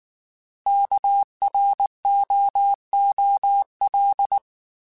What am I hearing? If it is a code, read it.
Morse code, heard as a keyed tone, about 19 wpm, 800 Hz. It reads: KROOL